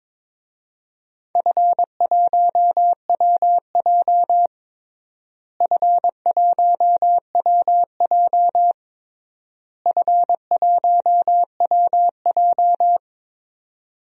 Morse code, heard as a keyed tone, about 22 wpm, 700 Hz. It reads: F1WJ F1WJ F1WJ
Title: DXCC-100 Call Signs - 4 Characters - Repeated 3x - 22wpm